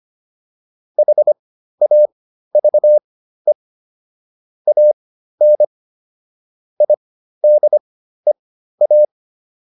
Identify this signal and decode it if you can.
Morse code, heard as a keyed tone, about 25 wpm, spaced as for 13 wpm, 600 Hz. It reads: HAVE AN IDEA